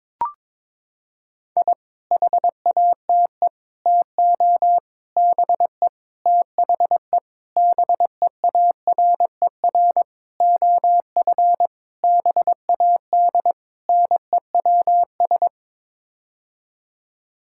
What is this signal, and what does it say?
Morse code, heard as a keyed tone, about 22 wpm, 700 Hz. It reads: I HATE TO BE THE BEARER OF BAD NEWS
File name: Sets of 9 Words - 22wpm